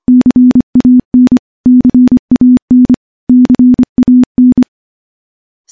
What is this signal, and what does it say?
Morse code, heard as a keyed tone, about 24 wpm, 250 Hz. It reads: CAN CAN CAN
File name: morse-training-25WPM